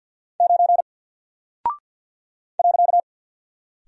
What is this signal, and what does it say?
Morse code, heard as a keyed tone, about 50 wpm, 700 Hz. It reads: 9 .
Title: Single Character (Letter:Number:Punctuation:Prosign) Alt 50wpm